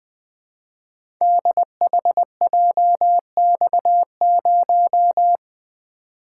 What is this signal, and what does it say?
Morse code, heard as a keyed tone, about 20 wpm, 700 Hz. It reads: DHJX0
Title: Code Groups - 5 Letter:Numbers - 20wpm